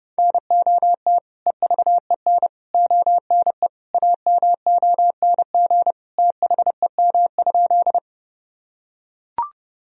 Morse code, heard as a keyed tone, about 31 wpm, 700 Hz. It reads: NOT EVEN ONE AMONG THEM?
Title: Sentences from Top 1000 Words - Review Part 3 - 30wpm